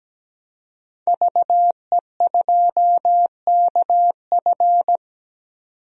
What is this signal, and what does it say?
Morse code, heard as a keyed tone, about 17 wpm, 700 Hz. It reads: VE2KF
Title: DXCC-100 Call Signs - 5 Characters - 17wpm